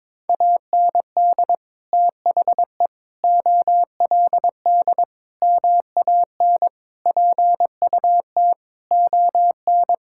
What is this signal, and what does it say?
Morse code, heard as a keyed tone, about 22 wpm, 700 Hz. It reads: AND THE OLD MAN PUT ON
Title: Sentences from Top 1600 Words 22wpm